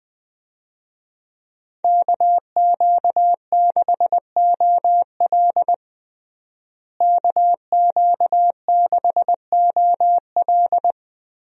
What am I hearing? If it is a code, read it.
Morse code, heard as a keyed tone, about 21 wpm, 700 Hz. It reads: KQ6OL KQ6OL
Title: USA Call Signs - 5 Characters - Repeated 3x - 20wpm